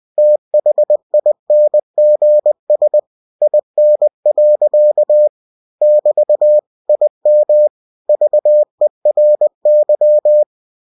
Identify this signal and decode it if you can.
Morse code, heard as a keyed tone, about 20 wpm, 600 Hz. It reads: THINGS IN. <BT> IM VERY
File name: cw-books / winnie-the-pooh / winnie-20wpm-page-0058